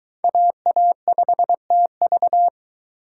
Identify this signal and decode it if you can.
Morse code, heard as a keyed tone, about 20 wpm, 700 Hz. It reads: AA5TV